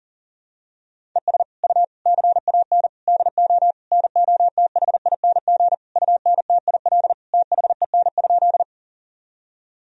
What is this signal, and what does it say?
Morse code, heard as a keyed tone, about 37 wpm, 700 Hz. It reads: ES U CAN DO NOTHING UNTIL THEN?
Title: Sentences from Top 500 Words and Abbreviations 40wpm